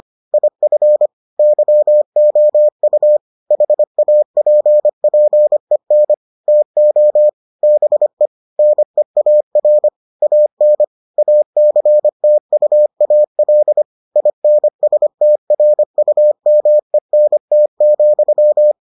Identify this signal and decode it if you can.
Morse code, heard as a keyed tone, about 25 wpm, 600 Hz. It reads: IF YOU HAPPEN TO BE NEAR AN ACTUAL INSTRUMENT,